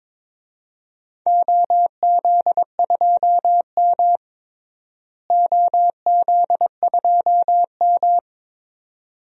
Morse code, heard as a keyed tone, about 22 wpm, 700 Hz. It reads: OZ2M OZ2M
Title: DXCC-100 Call Signs - 4 Characters - Repeated 2x - 22wpm